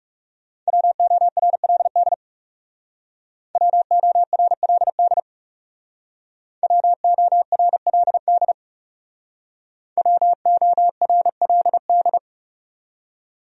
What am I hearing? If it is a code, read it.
Morse code, heard as a keyed tone, about 38 wpm, 700 Hz. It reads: WORLD WORLD WORLD WORLD